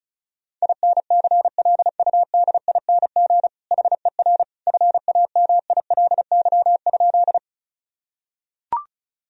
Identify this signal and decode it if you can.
Morse code, heard as a keyed tone, about 34 wpm, 700 Hz. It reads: INCLUDING HER FAMILY?
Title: Sentences from Top 1000 Words - Review Part 4 - 35wpm